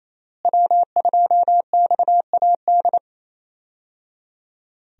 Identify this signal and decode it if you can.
Morse code, heard as a keyed tone, about 28 wpm, 700 Hz. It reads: W2XAD
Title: USA Call Signs - 5 Characters - Repeated 3x - 28wpm